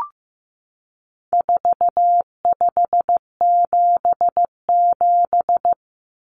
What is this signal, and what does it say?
Morse code, heard as a keyed tone, about 15 wpm, 700 Hz. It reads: 4577
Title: Common Cut Numbers - 4-Digit 15wpm